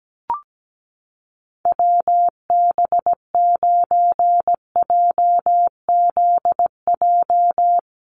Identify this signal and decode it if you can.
Morse code, heard as a keyed tone, about 17 wpm, 700 Hz. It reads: WB9JZJ
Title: USA Call Signs Repeated 3x 17wpm